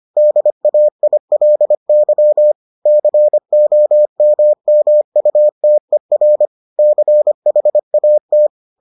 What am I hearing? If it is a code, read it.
Morse code, heard as a keyed tone, about 25 wpm, 600 Hz. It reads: DAILY COMMUTER CHAT